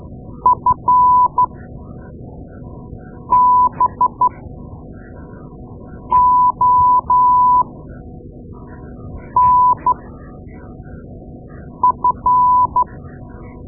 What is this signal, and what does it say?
Morse code, heard as a keyed tone, about 13 wpm, spaced as for 4 wpm, 1000 Hz. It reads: FBONF
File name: Morse Lessons / Morse Lessons / Tape-2A